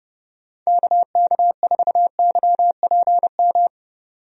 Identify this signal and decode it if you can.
Morse code, heard as a keyed tone, about 32 wpm, 700 Hz. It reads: KK4YPM
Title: USA Call Signs - 6 Characters - 30wpm